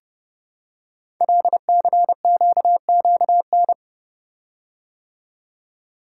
Morse code, heard as a keyed tone, about 30 wpm, 700 Hz. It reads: LCQQN
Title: Code Groups - 5 Letters - 30wpm